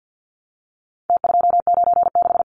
Morse code, heard as a keyed tone, about 50 wpm, 700 Hz. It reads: T296
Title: Numbers - Contest Style - 4-Digit 50wpm